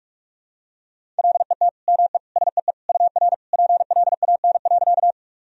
Morse code, heard as a keyed tone, about 41 wpm, 700 Hz. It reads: LET ME SEE UR PLAN.
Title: Sentences from Top 500 Words and Abbreviations 45wpm